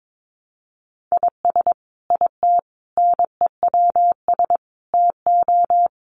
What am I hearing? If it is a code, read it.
Morse code, heard as a keyed tone, about 22 wpm, 700 Hz. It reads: IS IT NEWS TO
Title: Sentences from Top 800 Words 22wpm